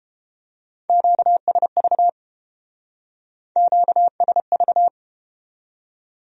Morse code, heard as a keyed tone, about 29 wpm, 700 Hz. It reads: QSV QSV